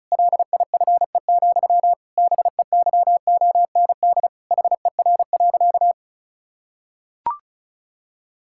Morse code, heard as a keyed tone, about 34 wpm, 700 Hz. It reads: LIFE, BEYOND HER.
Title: Sentences from Top 1000 Words - Review Part 6 - 35wpm